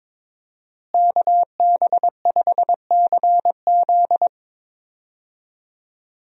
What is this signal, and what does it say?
Morse code, heard as a keyed tone, about 22 wpm, 700 Hz. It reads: KB5CZ
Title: USA Call Signs - 5 Characters - Repeated 2x - 22wpm